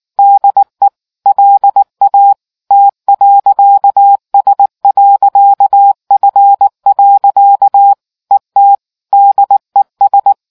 Morse code, heard as a keyed tone, about 19 wpm, 800 Hz. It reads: DE LA T.S.F. ET DES